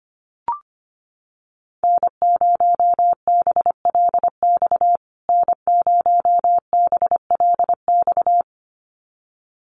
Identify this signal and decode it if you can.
Morse code, heard as a keyed tone, about 25 wpm, 700 Hz. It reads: N0BLX N0BLX
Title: USA Call Signs Repeated 2x 25wpm